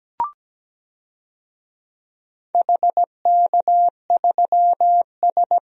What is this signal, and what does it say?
Morse code, heard as a keyed tone, about 17 wpm, 700 Hz. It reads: HK3S
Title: DXCC-100 Call Signs - 4 Characters - 17wpm